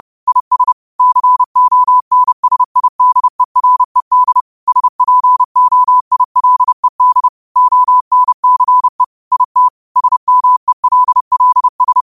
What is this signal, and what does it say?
Morse code, heard as a keyed tone, about 30 wpm, 1000 Hz. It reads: IS CONSIDERED SPOILED ONCE IT SMELLS